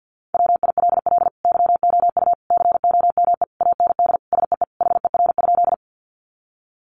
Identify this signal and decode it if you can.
Morse code, heard as a keyed tone, about 50 wpm, 700 Hz. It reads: WILL YOU COME AND SEE HER?
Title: Sentences from Top 100 Words 50wpm